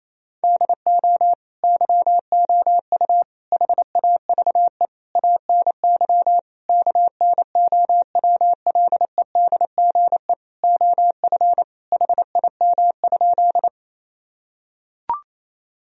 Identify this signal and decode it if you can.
Morse code, heard as a keyed tone, about 28 wpm, 700 Hz. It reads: DO YOU HAVE ANY KNOWLEDGE OF HIM?
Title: Sentences from Top 1000 Words - Review Part 3 - 28wpm